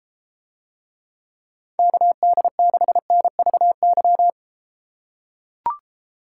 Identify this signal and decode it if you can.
Morse code, heard as a keyed tone, about 33 wpm, 700 Hz. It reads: KD6NVY